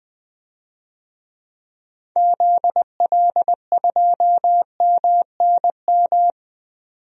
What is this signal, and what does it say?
Morse code, heard as a keyed tone, about 20 wpm, 700 Hz. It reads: ZL2MNM
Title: DXCC-100 Call Signs - 6 Characters - 20wpm